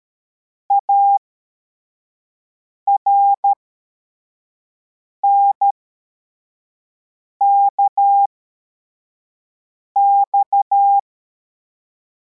Morse code, heard as a keyed tone, about 13 wpm, spaced as for 4 wpm, 800 Hz. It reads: ARNKX
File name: formation / cw / Text24